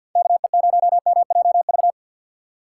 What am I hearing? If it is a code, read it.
Morse code, heard as a keyed tone, about 62 wpm, 700 Hz. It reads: KE0MJV
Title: Speed Racing - USA Call Signs 50wpm